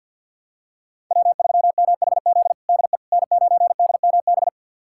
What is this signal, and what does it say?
Morse code, heard as a keyed tone, about 48 wpm, 690 Hz. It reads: W3MHZ DE N9DMB
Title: USA Callsign DE Callsign - 5 Characters - 50wpm